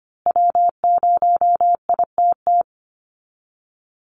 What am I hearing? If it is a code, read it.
Morse code, heard as a keyed tone, about 25 wpm, 700 Hz. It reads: W0ITT
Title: Call Signs Repeated 2x - 25wpm